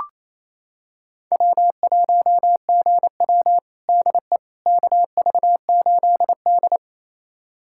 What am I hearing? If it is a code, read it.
Morse code, heard as a keyed tone, about 28 wpm, 700 Hz. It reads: W1GW DE KV8D
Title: USA Callsign DE Callsign - 4 Characters - 28wpm